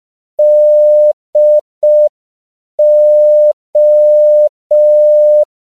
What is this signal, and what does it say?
Morse code, heard as a keyed tone, about 5 wpm, 600 Hz. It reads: DO
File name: CW / 5PPM